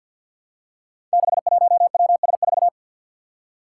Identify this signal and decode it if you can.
Morse code, heard as a keyed tone, about 53 wpm, 700 Hz. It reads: 61WS4
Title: Five Challenging Letter-Number Combinations 50wpm